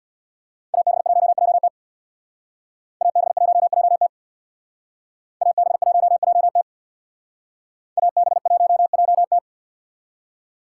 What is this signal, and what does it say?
Morse code, heard as a keyed tone, about 62 wpm, 700 Hz. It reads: AB1JT AB1JT AB1JT AB1JT
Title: Speed Racing - USA 2x2 Call Signs 50wpm